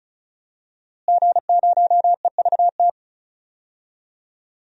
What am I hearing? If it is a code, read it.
Morse code, heard as a keyed tone, about 36 wpm, 700 Hz. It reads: G0EVT
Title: DXCC-100 Call Signs - 5 Characters - Repeated 3x - 35wpm